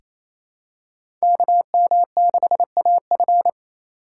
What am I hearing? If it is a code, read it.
Morse code, heard as a keyed tone, about 28 wpm, 700 Hz. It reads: KM6AF
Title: USA Call Signs - 5 Characters - 4x Character Spacing - 28wpm